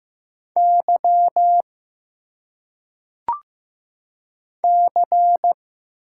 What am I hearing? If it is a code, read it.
Morse code, heard as a keyed tone, about 15 wpm, 700 Hz. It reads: Y C
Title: Single Letters - Phonetic Pronunciation 15wpm